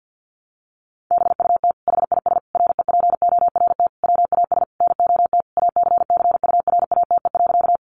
Morse code, heard as a keyed tone, about 50 wpm, 700 Hz. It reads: BUT HIS REPORT WAS NOT ACCURATE.
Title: Sentences from Top 1200 Words 50wpm